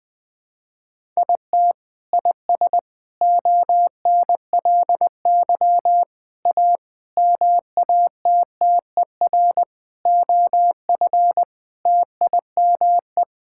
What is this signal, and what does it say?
Morse code, heard as a keyed tone, about 20 wpm, 700 Hz. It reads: IT IS ONLY A MATTER OF TIME
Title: Sets of 8 Words - 20wpm